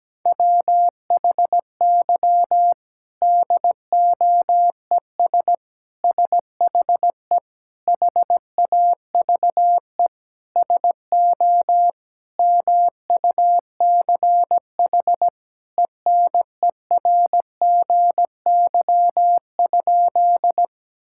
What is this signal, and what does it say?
Morse code, heard as a keyed tone, about 17 wpm, 700 Hz. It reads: WHY DOES SHE HAVE SO MUCH ENERGY?
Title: Sentences from Top 1000 Words - Review Part 6 - 17wpm